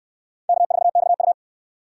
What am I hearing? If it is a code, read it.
Morse code, heard as a keyed tone, about 68 wpm, 700 Hz. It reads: K4XU